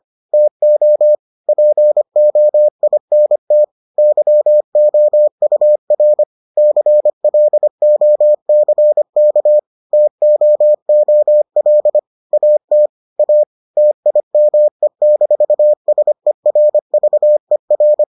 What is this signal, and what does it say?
Morse code, heard as a keyed tone, about 25 wpm, 600 Hz. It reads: TO POINT YOUR CLOCK TOOL AT A TIME-SERVER